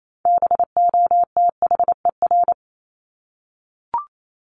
Morse code, heard as a keyed tone, about 28 wpm, 700 Hz. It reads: BOTHER
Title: Top 1501-1750 Words 28wpm